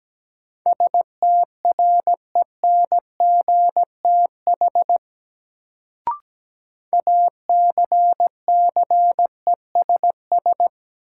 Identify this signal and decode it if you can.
Morse code, heard as a keyed tone, about 17 wpm, 700 Hz. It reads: STRENGTH ACCESS
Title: Sentences from Top 1000 Words - Review Part 4 - New Word Drill - 17wpm